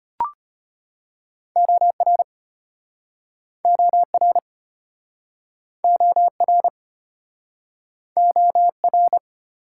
Morse code, heard as a keyed tone, about 34 wpm, 700 Hz. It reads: OR OR OR OR